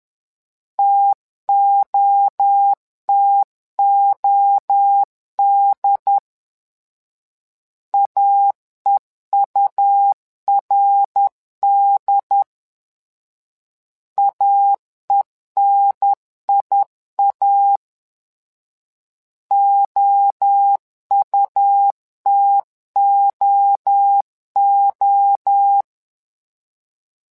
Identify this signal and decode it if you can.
Morse code, heard as a keyed tone, about 10 wpm, 800 Hz. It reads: TOTOD AEURD AENIA OUTOO
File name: LEZIONI / 3 MWRO / 3 MWRO 10wpm 2148